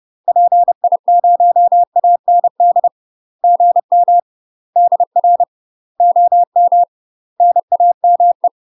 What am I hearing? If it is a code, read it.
Morse code, heard as a keyed tone, about 30 wpm, 700 Hz. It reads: PI0AND GM DR OM NAME
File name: UK MASTER QSOs TRAINING / 30WPM / UK MASTER QSOs 0020